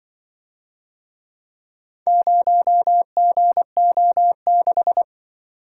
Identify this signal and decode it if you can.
Morse code, heard as a keyed tone, about 25 wpm, 700 Hz. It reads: 0GO6
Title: Speed-Racing - Four Letter-Number Combinations - 20wpm